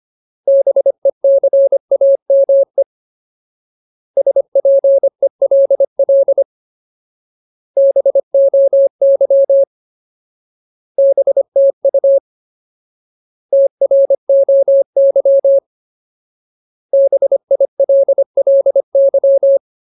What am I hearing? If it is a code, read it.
Morse code, heard as a keyed tone, about 25 wpm, 550 Hz. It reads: BECAME SPELL BOY BTU TROY BILLY
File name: Ultimate Word List / 25WPM Ultimate